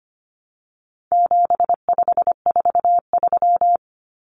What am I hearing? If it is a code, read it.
Morse code, heard as a keyed tone, about 25 wpm, 700 Hz. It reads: 7543
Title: Common Cut Numbers - 4-Digit 25wpm